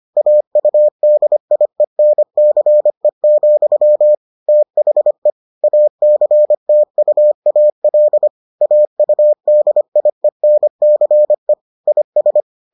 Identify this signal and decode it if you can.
Morse code, heard as a keyed tone, about 25 wpm, 600 Hz. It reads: AUDIENCE, THE ACTUAL AUDIENCE IS